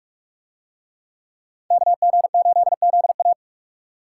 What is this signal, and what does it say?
Morse code, heard as a keyed tone, about 44 wpm, 700 Hz. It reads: KG8ZA